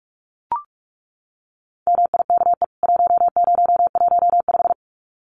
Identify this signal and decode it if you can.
Morse code, heard as a keyed tone, about 44 wpm, 700 Hz. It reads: MIKE 1015
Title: CWT Exchange 45wpm